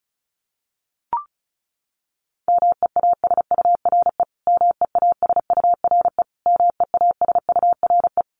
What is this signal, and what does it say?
Morse code, heard as a keyed tone, about 34 wpm, 700 Hz. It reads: MEASURE MEASURE MEASURE
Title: Top 500 Words Repeated 35wpm